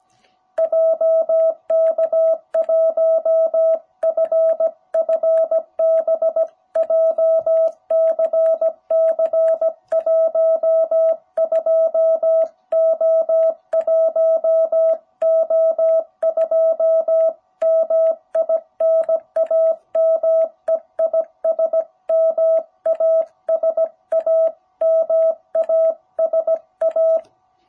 Morse code, heard as a keyed tone, about 17 wpm, 660 Hz. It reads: JK1FFB JCC 12O1O2 MI NAME IS MASA MASA
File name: jh8cbh / 2021cw